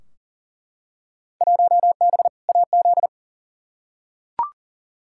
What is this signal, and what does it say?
Morse code, heard as a keyed tone, about 40 wpm, 700 Hz. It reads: 1B AZ
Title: Field Day Exchanges Encore - 40wpm